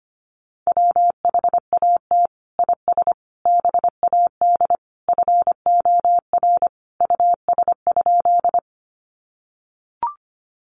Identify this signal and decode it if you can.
Morse code, heard as a keyed tone, about 25 wpm, 700 Hz. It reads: WHAT IS BAD FOR US?
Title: Sentences from Top 700 Words 25wpm